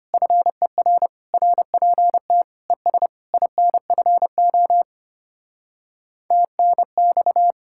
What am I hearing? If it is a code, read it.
Morse code, heard as a keyed tone, about 33 wpm, 700 Hz. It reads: FER RPT ES INFO TNX